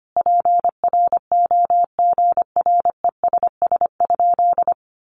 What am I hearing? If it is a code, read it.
Morse code, heard as a keyed tone, about 25 wpm, 700 Hz. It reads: PROGRESS?
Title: Sentences from Top 1400 Words 25wpm